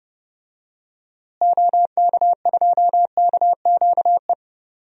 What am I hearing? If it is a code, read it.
Morse code, heard as a keyed tone, about 30 wpm, 700 Hz. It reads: OK2KQE
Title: DXCC-100 Call Signs - 6 Characters - Repeated 2x - 30wpm